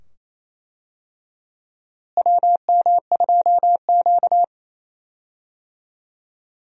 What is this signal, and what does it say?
Morse code, heard as a keyed tone, about 28 wpm, 700 Hz. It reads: WM2Q
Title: USA Call Signs - 4 Characters - Repeated 2x - 28wpm